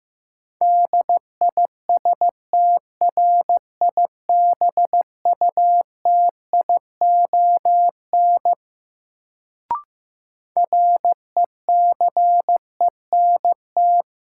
Morse code, heard as a keyed tone, about 15 wpm, 700 Hz. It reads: DISTRIBUTION RECENT
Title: Sentences from Top 1000 Words - Review Part 7 - New Word Drill - 15wpm